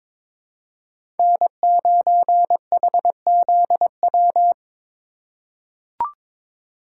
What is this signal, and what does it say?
Morse code, heard as a keyed tone, about 22 wpm, 700 Hz. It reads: N9HZW
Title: USA Call Signs - 5 Characters - 8x Character Spacing - 22wpm